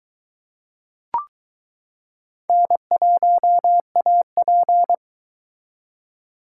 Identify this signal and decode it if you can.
Morse code, heard as a keyed tone, about 22 wpm, 700 Hz. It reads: N1AP